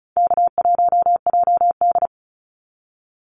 Morse code, heard as a keyed tone, about 34 wpm, 700 Hz. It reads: K1JD
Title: NA SOTA Activator Callsigns 35wpm